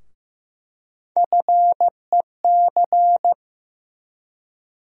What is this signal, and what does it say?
Morse code, heard as a keyed tone, about 16 wpm, 700 Hz. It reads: FEC
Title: USA Call Sign Postfixes - 3 Letters 15wpm